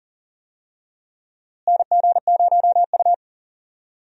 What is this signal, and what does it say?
Morse code, heard as a keyed tone, about 40 wpm, 700 Hz. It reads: NG0U
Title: USA Call Signs - 4 Characters - Repeated 3x - 40wpm